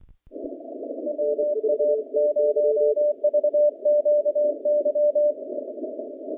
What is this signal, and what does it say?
Morse code, heard as a keyed tone, about 22 wpm, 590 Hz. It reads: WA0VQY
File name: contests / naqp-cw-k5tr-op-k5pi- -0300